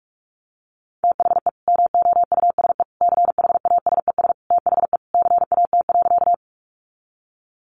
Morse code, heard as a keyed tone, about 44 wpm, 700 Hz. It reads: THE MOUSE CHASES THE CAT.
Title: Easy Phrases 45wpm